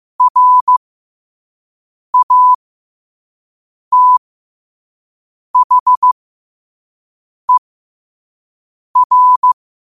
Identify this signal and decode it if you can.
Morse code, heard as a keyed tone, about 15 wpm, spaced as for 5 wpm, 1000 Hz. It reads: RATHER
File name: qotdfiles / 5wpm